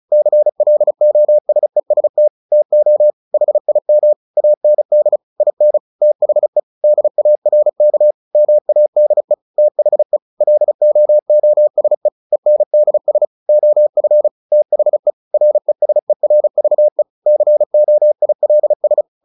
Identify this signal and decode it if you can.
Morse code, heard as a keyed tone, about 34 wpm, 600 Hz. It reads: CLOSEST TO HIM AND IN THE DARK MADE THE LOOSE ENDS OF THE RESERVE COILS
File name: the old man and the sea / the old man and the sea 1 35wpm